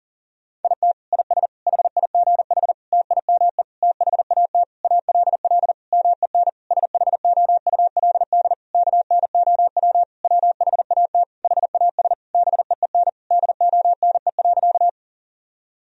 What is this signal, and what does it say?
Morse code, heard as a keyed tone, about 40 wpm, 700 Hz. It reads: IT IS HIGH TIME THAT ALL MEN SHOULD KNOW WHAT HAS BEEN DONE.
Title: Sentences from Top 1000 Words - Review Part 1 - 40wpm